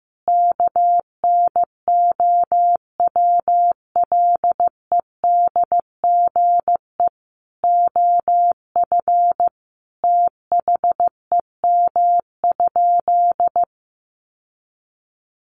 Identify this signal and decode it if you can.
Morse code, heard as a keyed tone, about 15 wpm, 700 Hz. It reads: KNOWLEDGE OF THEM?
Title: Sentences from Top 600 Words 15wpm